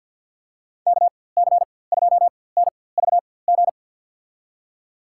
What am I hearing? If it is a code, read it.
Morse code, heard as a keyed tone, about 47 wpm, spaced as for 23 wpm, 700 Hz. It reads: KC2NVG
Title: USA Call Signs - 4x Character Spacing - 50wpm